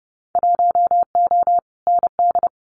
Morse code, heard as a keyed tone, about 30 wpm, 700 Hz. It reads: 1O ND
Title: Winter Field Day Exchanges 30wpm